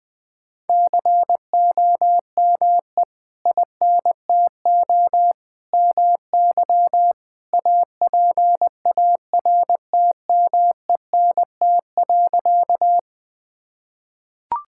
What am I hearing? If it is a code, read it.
Morse code, heard as a keyed tone, about 20 wpm, 700 Hz. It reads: COME INTO MY APARTMENT.
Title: Sentences from Top 1000 Words - Review Part 6 - 20wpm